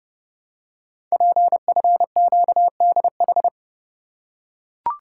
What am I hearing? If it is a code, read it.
Morse code, heard as a keyed tone, about 30 wpm, 700 Hz. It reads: PFQDH
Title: Code Groups - 5 Letters - 30wpm